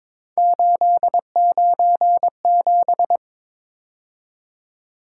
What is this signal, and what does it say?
Morse code, heard as a keyed tone, about 26 wpm, 700 Hz. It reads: 897
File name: Speed-Racing - Three Letter-Number Combinations - 22wpm